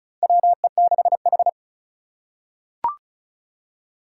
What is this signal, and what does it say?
Morse code, heard as a keyed tone, about 34 wpm, 700 Hz. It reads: WE6H
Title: USA Call Signs - 4 Characters - 2x Character Spacing - 35wpm